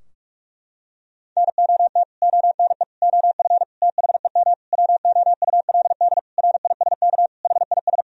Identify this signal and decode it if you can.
Morse code, heard as a keyed tone, about 42 wpm, 700 Hz. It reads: NOT ONE OF THEM WOULD RISK HIS
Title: Sentences from Top 1000 Words - Review Part 4 - 45wpm